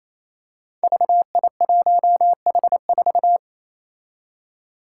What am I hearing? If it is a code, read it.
Morse code, heard as a keyed tone, about 28 wpm, 700 Hz. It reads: VI1H4
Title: Five Challenging Letter-Number Combinations 28wpm